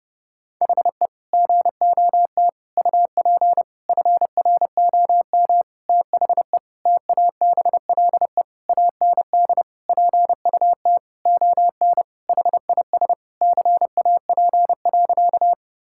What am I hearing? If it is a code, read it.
Morse code, heard as a keyed tone, about 30 wpm, 700 Hz. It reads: HE GOT UP FROM THE TABLE AND PUT ON HIS CAP.
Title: Sentences from Top 1600 Words 30wpm